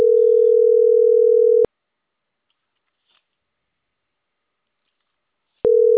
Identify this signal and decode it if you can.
Ringback tone (North America), 440 + 480 Hz.